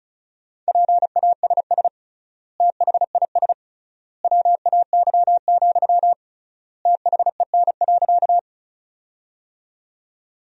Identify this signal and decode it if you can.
Morse code, heard as a keyed tone, about 36 wpm, 700 Hz. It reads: PASS THIS WAY, THEN.
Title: 3x Word-Spacing - Sentences from Top 400 Words - 35wpm